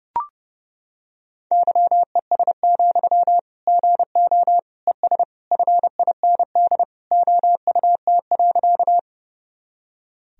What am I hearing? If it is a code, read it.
Morse code, heard as a keyed tone, about 30 wpm, 700 Hz. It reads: YES, GO ES FIND OUT.
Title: Sentences from Top 500 Words and Abbreviations 30wpm